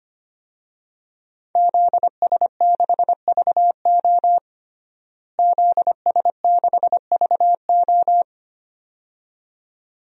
Morse code, heard as a keyed tone, about 25 wpm, 700 Hz. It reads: ZS6VO ZS6VO
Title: DXCC-100 Call Signs - 5 Characters - Repeated 2x - 25wpm